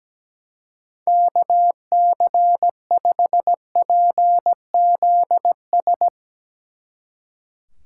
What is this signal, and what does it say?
Morse code, heard as a keyed tone, about 17 wpm, 700 Hz. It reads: KC5PZS
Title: Speed Racing - USA 2x3 Call Signs 17wpm